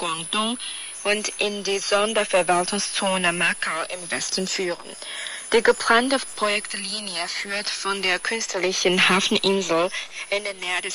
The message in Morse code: none